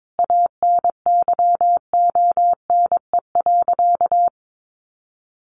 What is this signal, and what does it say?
Morse code, heard as a keyed tone, about 22 wpm, 700 Hz. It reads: ANYONE.
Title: Sentences from Top 1200 Words 22wpm